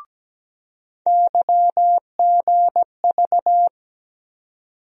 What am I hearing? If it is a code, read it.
Morse code, heard as a keyed tone, about 17 wpm, 700 Hz. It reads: YGV
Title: USA Call Sign Postfixes - 3 Letters 17wpm